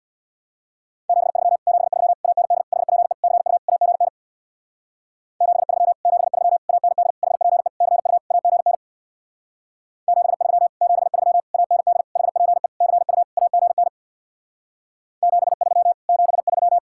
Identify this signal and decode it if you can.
Morse code, heard as a keyed tone, about 66 wpm, 700 Hz. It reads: 73 73 AND HPE CU AGN 73 73 AND HPE CU AGN 73 73 AND HPE CU AGN 73 73